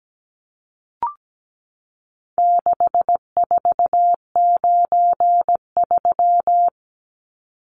Morse code, heard as a keyed tone, about 17 wpm, 700 Hz. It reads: 6493
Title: Numbers - 4-Digit 17wpm